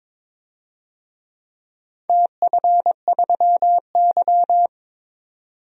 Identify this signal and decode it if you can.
Morse code, heard as a keyed tone, about 23 wpm, 700 Hz. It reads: TF3Y
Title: DXCC-100 Call Signs - 4 Characters - 22wpm